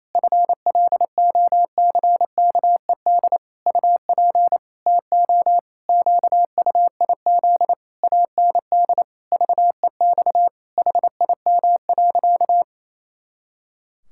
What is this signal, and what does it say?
Morse code, heard as a keyed tone, about 27 wpm, 700 Hz. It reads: FLOCKED UP TO QUIZ AND VEX HIM.
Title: Pangram Challenge v2 - 28wpm